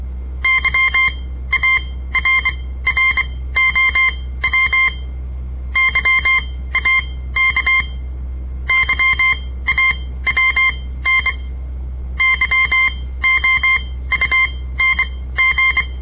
Morse code, heard as a keyed tone, about 23 wpm, spaced as for 15 wpm, 2100 Hz. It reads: YARROW YAK YAWN YOUNG
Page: On the air Morse code practice from the Southern Montana Amatuer Radio Association; all code, no read back